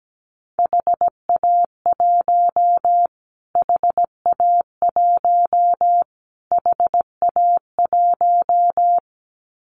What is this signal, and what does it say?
Morse code, heard as a keyed tone, about 17 wpm, 700 Hz. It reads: HA1 HA1 HA1
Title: Callsign Prefixes for Top 25 DXCC - Repeated 3x - 17wpm